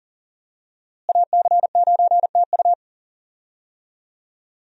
Morse code, heard as a keyed tone, about 40 wpm, 700 Hz. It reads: AC9TU
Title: USA Call Signs - 5 Characters - Repeated 2x - 40wpm